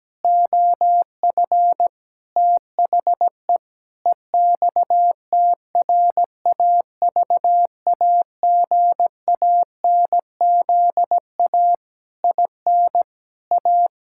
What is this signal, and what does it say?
Morse code, heard as a keyed tone, about 17 wpm, 700 Hz. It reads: OF THE EXTRAVAGANZA IN A
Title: Pangram Challenge v2 - 17wpm